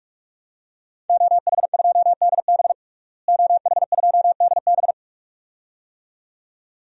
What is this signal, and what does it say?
Morse code, heard as a keyed tone, about 46 wpm, 700 Hz. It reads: OH2DB OH2DB
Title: DXCC-100 Call Signs - 5 Characters - Repeated 2x - 45wpm